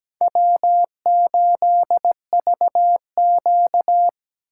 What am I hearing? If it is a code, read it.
Morse code, heard as a keyed tone, about 20 wpm, 700 Hz. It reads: W8VQ
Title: Speed Racing - USA 1x2 Call Signs 17wpm